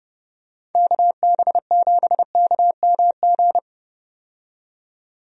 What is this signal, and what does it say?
Morse code, heard as a keyed tone, about 31 wpm, 700 Hz. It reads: KB7KMG
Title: USA Call Signs - 6 Characters - 30wpm